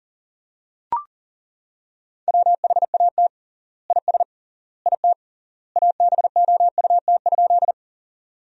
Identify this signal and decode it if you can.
Morse code, heard as a keyed tone, about 40 wpm, 700 Hz. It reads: WHAT IS IT ABOUT?
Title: 3x Word-Spacing - Sentences from Top 100 Words - 40wpm